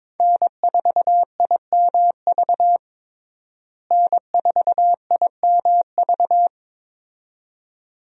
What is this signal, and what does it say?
Morse code, heard as a keyed tone, about 22 wpm, 700 Hz. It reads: N4IMV N4IMV
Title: USA Call Signs - 5 Characters - Repeated 2x - 22wpm